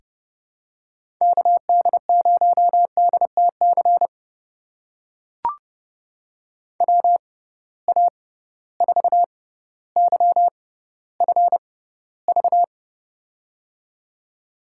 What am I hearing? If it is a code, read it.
Morse code, heard as a keyed tone, about 32 wpm, spaced as for 16 wpm, 700 Hz. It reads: KD0DTC WA4YFV
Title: USA Call Signs - 6 Characters - 6x Character Spacing - 30wpm